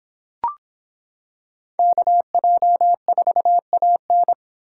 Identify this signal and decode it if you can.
Morse code, heard as a keyed tone, about 26 wpm, 700 Hz. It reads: KJ4AN